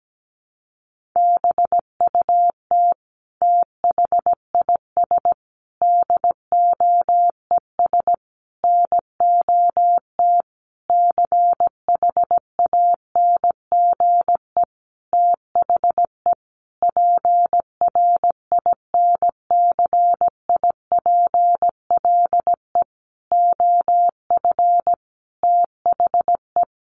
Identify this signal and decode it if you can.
Morse code, heard as a keyed tone, about 17 wpm, 700 Hz. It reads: BUT THIS DOES NOT CHANGE THE PRINCIPLE OF THE
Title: Sentences from Top 1400 Words 17wpm